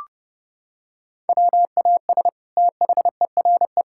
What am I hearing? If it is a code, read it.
Morse code, heard as a keyed tone, about 32 wpm, 700 Hz. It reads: WAS THERE
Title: Sentences from Top 1000 Words - Review Part 6 - 30wpm